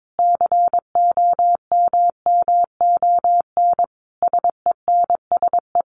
Morse code, heard as a keyed tone, about 22 wpm, 700 Hz. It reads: COMMON SENSE
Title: Sentences from Top 1200 Words 22wpm